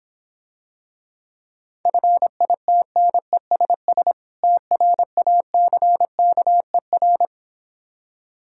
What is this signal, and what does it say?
Morse code, heard as a keyed tone, about 26 wpm, 700 Hz. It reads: FITNESS TRACKER